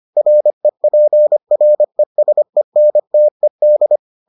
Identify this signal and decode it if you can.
Morse code, heard as a keyed tone, about 25 wpm, 600 Hz. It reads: REPRESENTED